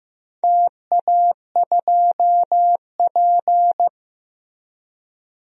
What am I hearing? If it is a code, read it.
Morse code, heard as a keyed tone, about 15 wpm, 700 Hz. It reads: TA2P
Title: DXCC-100 Call Signs - 4 Characters - 15wpm